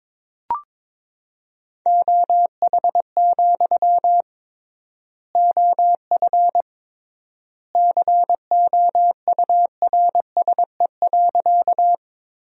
Morse code, heard as a keyed tone, about 22 wpm, 700 Hz. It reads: OH, OF COURSE.
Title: 3x Word-Spacing - Sentences from Top 500 Words - 22wpm